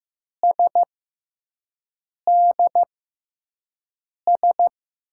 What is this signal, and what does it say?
Morse code, heard as a keyed tone, about 15 wpm, spaced as for 5 wpm, 700 Hz. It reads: SDS
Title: USA Call Signs - 6x Character Spacing - 15wpm